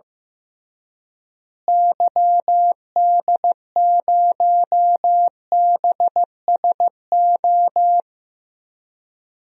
Morse code, heard as a keyed tone, about 16 wpm, 700 Hz. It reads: YD0BSO
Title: DXCC-100 Call Signs - 6 Characters - Repeated 3x - 15wpm